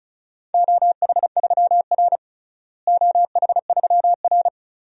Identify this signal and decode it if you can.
Morse code, heard as a keyed tone, about 36 wpm, 700 Hz. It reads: OH3R OH3R
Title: DXCC-100 Call Signs - 4 Characters - Repeated 2x - 35wpm